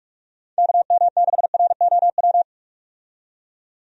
Morse code, heard as a keyed tone, about 43 wpm, 700 Hz. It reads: KM6ROW
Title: USA Call Signs - 6 Characters - 45wpm